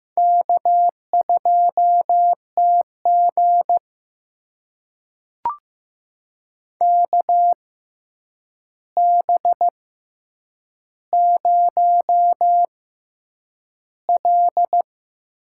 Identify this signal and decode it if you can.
Morse code, heard as a keyed tone, about 15 wpm, spaced as for 8 wpm, 700 Hz. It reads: K2TG KB0L
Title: USA Call Signs - 4 Characters - 6x Character Spacing - 15wpm